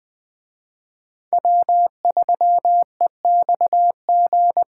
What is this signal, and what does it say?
Morse code, heard as a keyed tone, about 23 wpm, 700 Hz. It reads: W3EXG
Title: Speed Racing - USA 1x3 Call Signs 20wpm